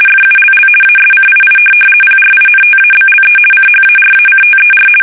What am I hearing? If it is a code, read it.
RTTY, about 50 baud. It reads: FULL ASCII PRINTABLE CHARACTER
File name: fsk850-50B / teletype-corp